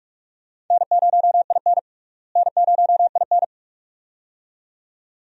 Morse code, heard as a keyed tone, about 46 wpm, 700 Hz. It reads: N0IN N0IN
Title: USA Call Signs - 4 Characters - Repeated 2x - 45wpm